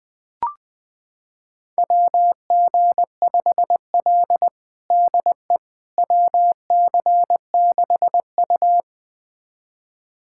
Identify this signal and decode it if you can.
Morse code, heard as a keyed tone, about 21 wpm, 700 Hz. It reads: WG5L DE WC6U
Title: USA Callsign DE Callsign - 4 Characters - 20wpm